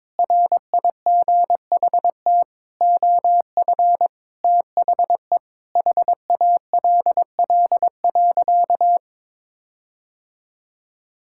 Morse code, heard as a keyed tone, about 22 wpm, 700 Hz. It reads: RIGHT OF THE HALL.
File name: practice / 22wpm / Sentences with 8-Word Phrases - 22wpm